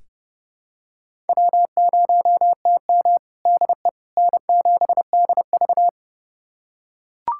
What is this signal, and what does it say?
Morse code, heard as a keyed tone, about 30 wpm, 700 Hz. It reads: W0TM DE N7DV
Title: USA Callsign DE Callsign - 4 Characters - 30wpm